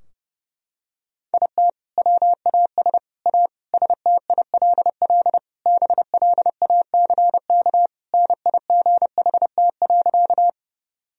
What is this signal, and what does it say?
Morse code, heard as a keyed tone, about 32 wpm, 700 Hz. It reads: IT WAS A STILL BLACK NIGHT.
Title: Sentences from Top 1000 Words - Review Part 2 - 30wpm